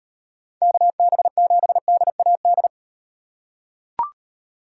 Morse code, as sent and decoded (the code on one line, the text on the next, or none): -.- -... --... -.. .- -..
KB7DAD